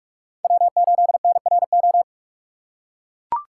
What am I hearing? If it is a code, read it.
Morse code, heard as a keyed tone, about 44 wpm, 700 Hz. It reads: W8NRO